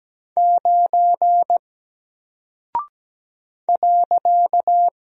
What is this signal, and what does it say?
Morse code, heard as a keyed tone, about 17 wpm, 700 Hz. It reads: 9 .
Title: Single Character (Letter:Number:Punctuation:Prosign) 17wpm